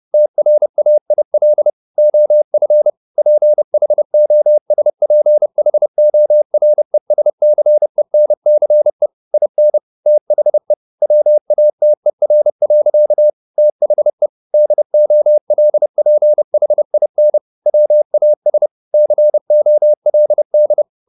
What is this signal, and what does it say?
Morse code, heard as a keyed tone, about 30 wpm, 600 Hz. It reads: TRAIL OF PHOSPHORESCENCE IN THE WATER. THE DOLPHIN WAS COLD